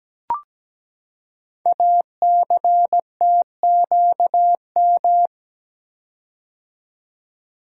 Morse code, heard as a keyed tone, about 17 wpm, 700 Hz. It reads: ACTQM
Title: Code Groups - 5 Letters - 17wpm